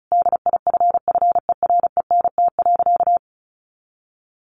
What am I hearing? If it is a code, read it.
Morse code, heard as a keyed tone, about 34 wpm, 700 Hz. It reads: DIFFERENT.
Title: Sentences from Top 1300 Words 35wpm